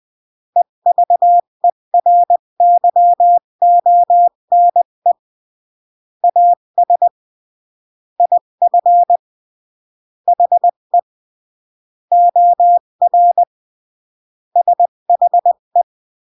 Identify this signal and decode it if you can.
Morse code, heard as a keyed tone, about 20 wpm, 700 Hz. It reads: EVERYONE AS IF HE OR SHE
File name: CW / jodorowsky / 82 commandments 20-15